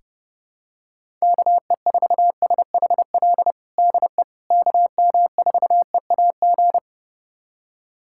Morse code, heard as a keyed tone, about 30 wpm, 700 Hz. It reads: KE4SHL DE KM4EAG
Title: USA Callsign DE Callsign - 6 Characters - 30wpm